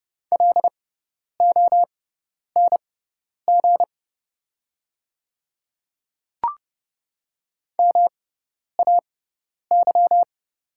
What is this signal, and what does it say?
Morse code, heard as a keyed tone, about 30 wpm, spaced as for 10 wpm, 700 Hz. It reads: LONG MAY
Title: 6x Character Spacing - Top 100 Words - 30wpm